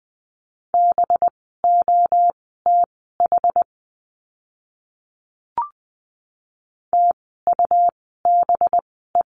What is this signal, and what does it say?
Morse code, heard as a keyed tone, about 20 wpm, spaced as for 14 wpm, 700 Hz. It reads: BOTH TUBE
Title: 185 Lesson 30f - Letter b - Words 20wpm